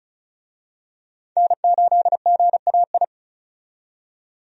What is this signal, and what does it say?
Morse code, heard as a keyed tone, about 36 wpm, 700 Hz. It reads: N8GAI
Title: USA Call Signs - 5 Characters - 35wpm